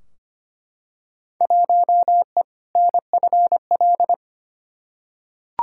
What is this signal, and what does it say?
Morse code, heard as a keyed tone, about 25 wpm, 700 Hz. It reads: 1E NFL
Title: Field Day Exchanges - 25wpm